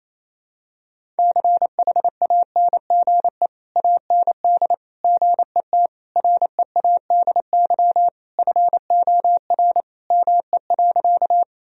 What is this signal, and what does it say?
Morse code, heard as a keyed tone, about 28 wpm, 700 Hz. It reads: CHANGE AND GET READY FOR ME.
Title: Sentences from Top 1000 Words - Review Part 1 - 28wpm